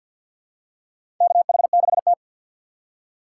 Morse code, heard as a keyed tone, about 52 wpm, 700 Hz. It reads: KH6T
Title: USA Call Signs - 4 Characters - 6x Character Spacing - 50wpm